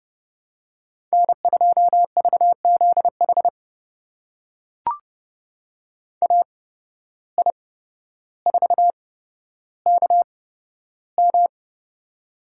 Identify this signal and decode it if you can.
Morse code, heard as a keyed tone, about 30 wpm, spaced as for 13 wpm, 700 Hz. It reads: N2VZH AI4KM
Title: USA Call Signs - 5 Characters - 8x Character Spacing - 30wpm